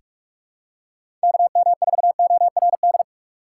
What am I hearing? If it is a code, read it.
Morse code, heard as a keyed tone, about 45 wpm, 690 Hz. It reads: KM4ORD
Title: USA Call Signs - 6 Characters - 2x Character Spacing - 45wpm